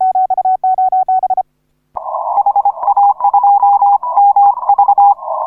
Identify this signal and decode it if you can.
Morse code, heard as a keyed tone, about 25 wpm, 810 Hz. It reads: HA2MV